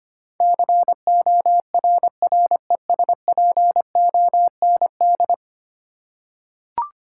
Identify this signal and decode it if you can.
Morse code, heard as a keyed tone, about 25 wpm, 700 Hz. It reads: CORRESPOND
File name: Ten Letter Words - 25wpm